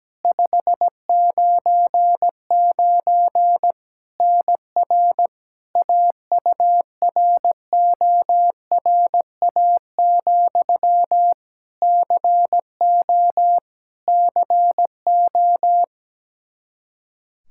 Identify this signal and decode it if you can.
Morse code, heard as a keyed tone, about 17 wpm, 700 Hz. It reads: 599 NR AURORA, CO CO
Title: K1USN SST QSO Practice 17wpm